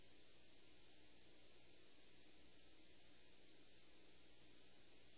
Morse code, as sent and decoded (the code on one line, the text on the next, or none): none